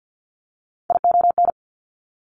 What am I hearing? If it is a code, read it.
Morse code, heard as a keyed tone, about 50 wpm, 700 Hz. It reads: ION
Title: Top Three Letter Combinations in Words 50wpm